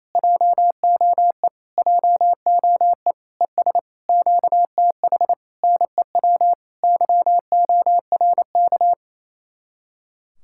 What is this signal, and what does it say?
Morse code, heard as a keyed tone, about 29 wpm, 700 Hz. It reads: JOE JOE ES QTH NEW YORK
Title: Speed-Racing Common QSO Phrases Encore - 28wpm